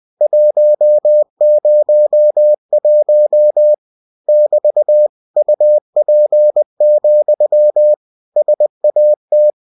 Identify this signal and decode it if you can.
Morse code, heard as a keyed tone, about 20 wpm, 600 Hz. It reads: 101 <BT> UP, SAT
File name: cw-books / winnie-the-pooh / winnie-20wpm-page-0101